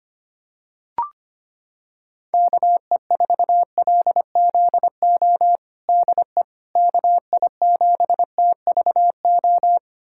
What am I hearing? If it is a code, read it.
Morse code, heard as a keyed tone, about 25 wpm, 700 Hz. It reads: KE4LZO DE KI7TVO
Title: USA Callsign DE Callsign - 6 Characters - 25wpm